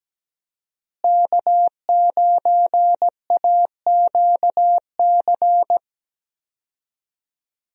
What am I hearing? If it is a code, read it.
Morse code, heard as a keyed tone, about 17 wpm, 700 Hz. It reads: K9AQC
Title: USA Call Signs - 5 Characters - Repeated 2x - 17wpm